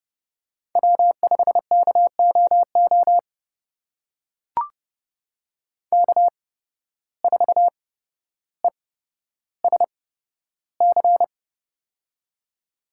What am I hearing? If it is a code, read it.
Morse code, heard as a keyed tone, about 30 wpm, spaced as for 13 wpm, 700 Hz. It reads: W5KOO K4ESC
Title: USA Call Signs - 5 Characters - 8x Character Spacing - 30wpm